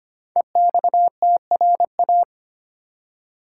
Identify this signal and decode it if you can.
Morse code, heard as a keyed tone, about 25 wpm, 700 Hz. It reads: EXTRA
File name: 5 Letter Words - First 3 Letters Given - 25wpm